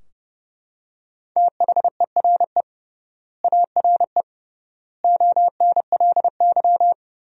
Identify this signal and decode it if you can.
Morse code, heard as a keyed tone, about 32 wpm, 700 Hz. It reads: THERE ARE ONLY
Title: 3x Word-Spacing - Sentences from Top 200 Words - 30wpm